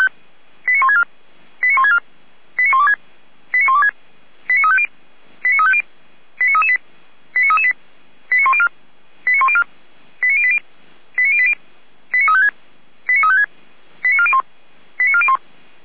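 Five-tone selective call (ZVEI), heard as quick runs of five tones, tones of about 70 ms. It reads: 89164 89164 89126 89126 89350 89350 89338 89338 89104 89104 89080 89080 89356 89356 89401 89401
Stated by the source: ZVEI five-tone call 89701 received on an SDR.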